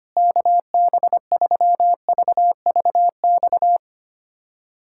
Morse code, heard as a keyed tone, about 25 wpm, 700 Hz. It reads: KB3VVX
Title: USA Callsign DE Callsign 25wpm